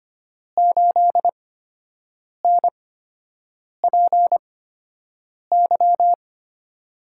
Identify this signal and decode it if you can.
Morse code, heard as a keyed tone, about 25 wpm, spaced as for 7 wpm, 700 Hz. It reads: 8NPY
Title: USA Call Signs - 6 Characters - 8x Character Spacing - 25wpm